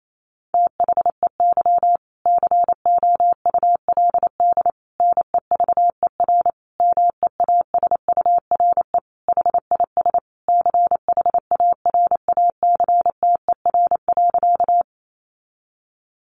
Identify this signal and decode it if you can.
Morse code, heard as a keyed tone, about 28 wpm, 700 Hz. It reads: THEY COULD NEVER MEASURE HIS CHARACTER.
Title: Sentences from Top 1200 Words 28wpm